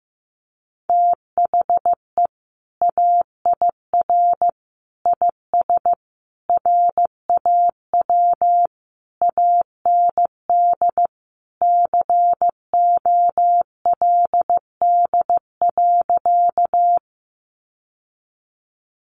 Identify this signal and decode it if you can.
Morse code, heard as a keyed tone, about 15 wpm, 700 Hz. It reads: THE AIR IS RAW AND COLD.
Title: Sentences from Top 1500 Words 15wpm